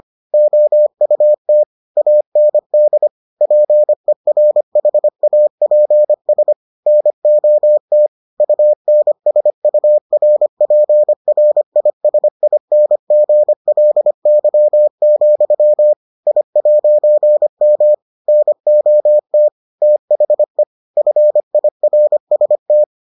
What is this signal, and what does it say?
Morse code, heard as a keyed tone, about 25 wpm, 600 Hz. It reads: OUT AND PERHAPS NOT UNSURPRISINGLY, I'M NOT THE FIRST